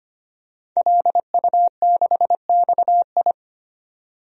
Morse code, heard as a keyed tone, about 25 wpm, 700 Hz. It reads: LU6XI
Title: DXCC-100 Call Signs - 5 Characters - 25wpm